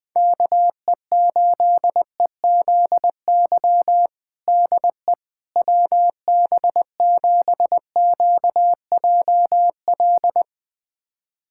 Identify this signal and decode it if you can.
Morse code, heard as a keyed tone, about 21 wpm, 700 Hz. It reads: KE8EZY DE WB7QJL
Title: USA Callsign DE Callsign - 6 Characters - 20wpm